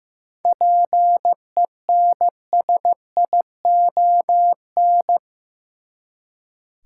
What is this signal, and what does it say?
Morse code, heard as a keyed tone, about 15 wpm, 700 Hz. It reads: PENSION
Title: Seven Letter Words - 15wpm